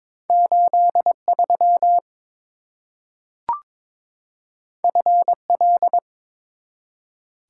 Morse code, heard as a keyed tone, about 23 wpm, 700 Hz. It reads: 83 FL
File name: practice / 22wpm / Two Letter-Number Combinations - 22wpm